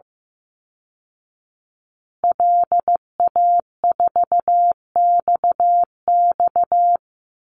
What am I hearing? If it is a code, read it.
Morse code, heard as a keyed tone, about 15 wpm, 700 Hz. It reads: LA4XX
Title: Call Signs Repeated 2x - 15wpm